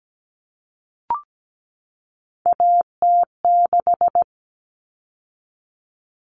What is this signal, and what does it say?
Morse code, heard as a keyed tone, about 17 wpm, 700 Hz. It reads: AT6
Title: All Cut Numbers - 3-Digit 17wpm